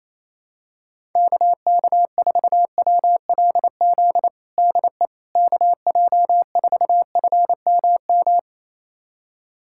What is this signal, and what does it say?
Morse code, heard as a keyed tone, about 28 wpm, 700 Hz. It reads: KK4WLZ DE KJ4FMM
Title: USA Callsign DE Callsign - 6 Characters - 28wpm